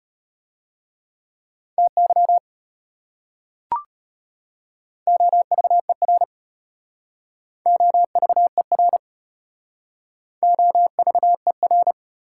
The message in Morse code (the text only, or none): TY OVER OVER OVER